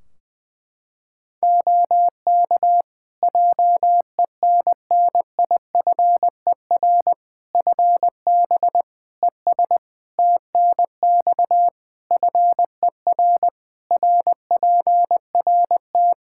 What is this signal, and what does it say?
Morse code, heard as a keyed tone, about 21 wpm, 700 Hz. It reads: OK JENNIFER FB ES TNX FER RPRT
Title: Practice QSO - Exchange 2 - Info - 20wpm